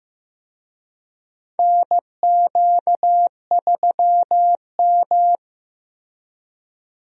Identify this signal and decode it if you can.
Morse code, heard as a keyed tone, about 15 wpm, 700 Hz. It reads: NQ3M
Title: USA Call Signs - 4 Characters - Repeated 2x - 15wpm